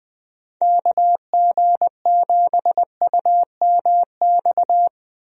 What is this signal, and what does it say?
Morse code, heard as a keyed tone, about 21 wpm, 700 Hz. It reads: KG7UMX
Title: USA Call Signs - 6 Characters - Repeated 3x - 20wpm